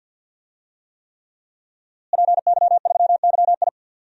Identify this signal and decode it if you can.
Morse code, heard as a keyed tone, about 47 wpm, 700 Hz. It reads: PY3YI
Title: DXCC-100 Call Signs - 5 Characters - Repeated 3x - 50wpm